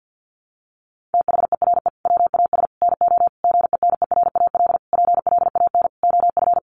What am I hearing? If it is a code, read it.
Morse code, heard as a keyed tone, about 50 wpm, 700 Hz. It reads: THERE WAS NO GENERAL PLAN OF
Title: Sentences from Top 700 Words 50wpm